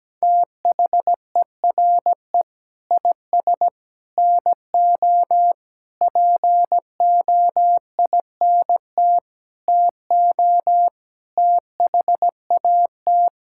Morse code, heard as a keyed tone, about 17 wpm, 700 Hz. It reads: THERE IS NO POINT TO THAT
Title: Sentences from Top 1000 Words - Review Part 5 - 17wpm